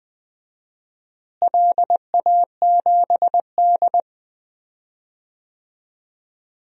Morse code, heard as a keyed tone, about 20 wpm, 700 Hz. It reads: LA7D
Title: DXCC-100 Call Signs - 4 Characters - Repeated 2x - 20wpm